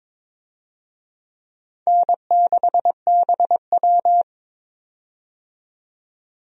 Morse code, heard as a keyed tone, about 22 wpm, 700 Hz. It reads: N6BW
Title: USA Call Signs - 4 Characters - 22wpm